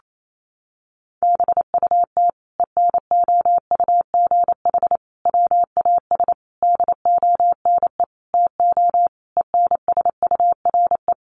Morse code, heard as a keyed tone, about 28 wpm, 700 Hz. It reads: BUT ENOUGH WAS DONE TO ENSURE
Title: Sentences from Top 1100 Words 28wpm